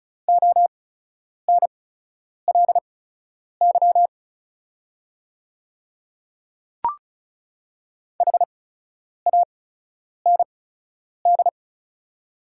Morse code, heard as a keyed tone, about 36 wpm, spaced as for 10 wpm, 690 Hz. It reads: ONLY HAND
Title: 8x Character Spacing - Top 200 Words - 35wpm